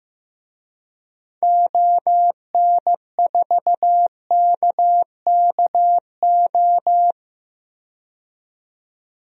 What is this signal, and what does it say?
Morse code, heard as a keyed tone, about 16 wpm, 700 Hz. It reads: ON4KKO
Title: DXCC-100 Call Signs - 6 Characters - 15wpm